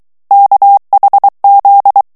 Morse code, heard as a keyed tone, about 23 wpm, 800 Hz. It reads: KHZ
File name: ANGLAIS texte clair 3 vitesse 22 mots minute